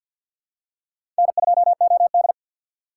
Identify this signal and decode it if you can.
Morse code, heard as a keyed tone, about 48 wpm, 700 Hz. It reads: N2OD
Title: USA Call Signs - 4 Characters - 3x Character Spacing - 50wpm